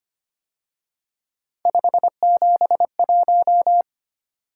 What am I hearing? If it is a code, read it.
Morse code, heard as a keyed tone, about 30 wpm, 700 Hz. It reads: 571
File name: Speed-Racing - Three Letter-Number Combinations - 25wpm